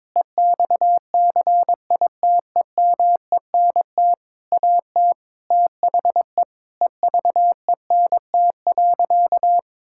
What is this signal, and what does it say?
Morse code, heard as a keyed tone, about 21 wpm, 700 Hz. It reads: EXCITEMENT AT THE EVENT.
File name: Sentences with 7-Word Phrases - 22wpm